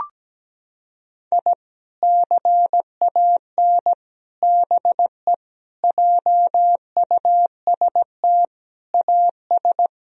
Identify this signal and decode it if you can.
Morse code, heard as a keyed tone, about 17 wpm, 700 Hz. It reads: I CAN BE JUST AS
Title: Sentences from Top 1000 Words - Review Part 3 - 17wpm